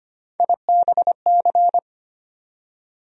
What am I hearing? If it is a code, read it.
Morse code, heard as a keyed tone, about 25 wpm, 700 Hz. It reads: IBC
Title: USA Call Sign Postfixes - 3 Letters 25wpm